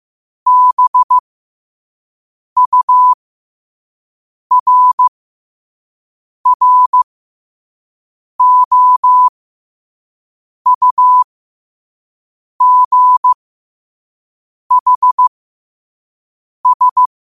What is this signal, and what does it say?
Morse code, heard as a keyed tone, about 15 wpm, spaced as for 5 wpm, 1000 Hz. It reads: BURROUGHS
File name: qotdfiles / 5wpm